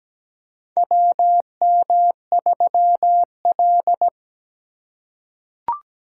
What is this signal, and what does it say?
Morse code, heard as a keyed tone, about 17 wpm, 700 Hz. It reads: WM3L
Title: USA Call Signs - 4 Characters - 4x Character Spacing - 17wpm